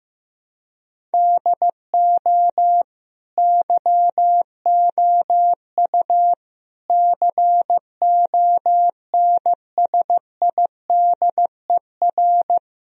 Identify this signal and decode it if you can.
Morse code, heard as a keyed tone, about 15 wpm, 700 Hz. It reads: DO YOU CONSIDER
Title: Sentences from Top 1000 Words - Review Part 4 - 15wpm